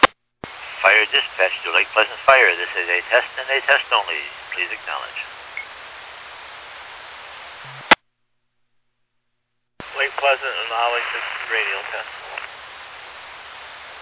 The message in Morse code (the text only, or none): none